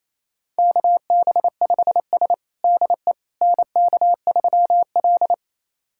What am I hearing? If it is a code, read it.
Morse code, heard as a keyed tone, about 28 wpm, 700 Hz. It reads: KB5S DE NK3L
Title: USA Callsign DE Callsign - 4 Characters - 28wpm